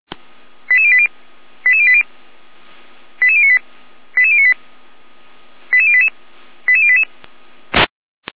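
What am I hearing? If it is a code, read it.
Five-tone selective call (ZVEI), heard as quick runs of five tones, tones of about 80 ms. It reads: 88080 88080 88098 88098 88088 88088
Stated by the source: ZVEI five-tone selective call received on SDR, channel 455, code 88571.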